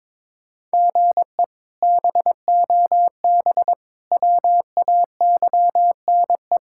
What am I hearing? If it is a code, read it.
Morse code, heard as a keyed tone, about 22 wpm, 700 Hz. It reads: GE BOB WAYNE
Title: K1USN SST QSO Practice - 22wpm